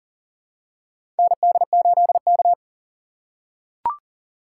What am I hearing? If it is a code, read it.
Morse code, heard as a keyed tone, about 40 wpm, 700 Hz. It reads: ND8K
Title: USA Call Signs - 4 Characters - 3x Character Spacing - 40wpm